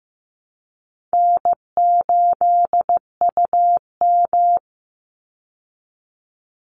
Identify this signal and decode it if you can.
Morse code, heard as a keyed tone, about 14 wpm, 700 Hz. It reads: N8UM
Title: Easy Call Signs Repeated 2x - 15wpm